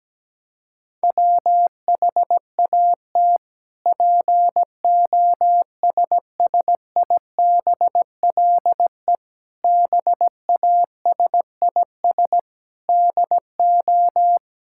Morse code, heard as a keyed tone, about 17 wpm, 700 Hz. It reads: WHAT POSSIBLE BASIS DO
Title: Sentences from Top 1000 Words - Review Part 5 - 17wpm